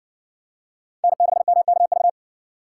Morse code, heard as a keyed tone, about 54 wpm, 700 Hz. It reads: N6MKV